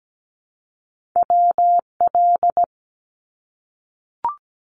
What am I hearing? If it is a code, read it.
Morse code, heard as a keyed tone, about 17 wpm, 700 Hz. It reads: WL
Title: Abbreviations Repeated 3x - 17wpm